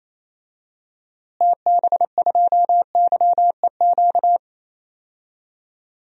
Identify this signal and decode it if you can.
Morse code, heard as a keyed tone, about 27 wpm, 700 Hz. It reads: TB2YEQ
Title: DXCC-100 Call Signs - 6 Characters - Repeated 2x - 28wpm